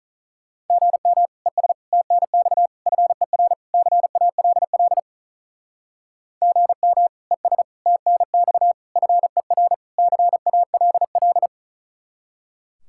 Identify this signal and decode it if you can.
Morse code, heard as a keyed tone, about 44 wpm, 700 Hz. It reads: GM ES TNX FER CALL GM ES TNX FER CALL
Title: Speed-Racing Common QSO Phrases Encore - 35wpm